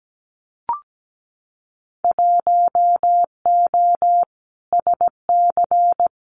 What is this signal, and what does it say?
Morse code, heard as a keyed tone, about 17 wpm, 700 Hz. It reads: 1O SC
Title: Winter Field Day Exchanges 17wpm